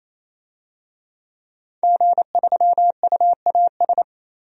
Morse code, heard as a keyed tone, about 28 wpm, 700 Hz. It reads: G3UAS
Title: DXCC-100 Call Signs - 5 Characters - 28wpm